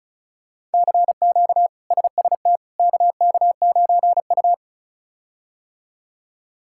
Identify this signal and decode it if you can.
Morse code, heard as a keyed tone, about 36 wpm, 700 Hz. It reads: CQ SST KK9U
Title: K1USN SST QSO Practice - 35wpm